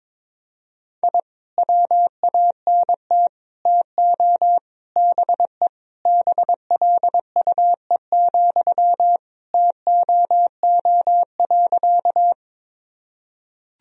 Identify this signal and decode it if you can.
Morse code, heard as a keyed tone, about 22 wpm, 700 Hz. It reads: I WANT TO BE BLUE, TOO.
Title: Sentences from Top 1000 Words - Review Part 3 - 22wpm